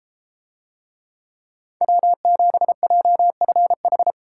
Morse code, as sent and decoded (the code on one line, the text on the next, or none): .-- --... .--- ..-. ....
W7JFH